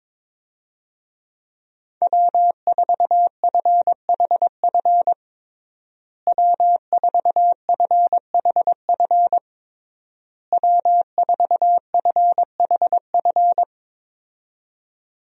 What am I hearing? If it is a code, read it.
Morse code, heard as a keyed tone, about 22 wpm, 700 Hz. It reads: W4FHF W4FHF W4FHF
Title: USA Call Signs - 5 Characters - Repeated 3x - 22wpm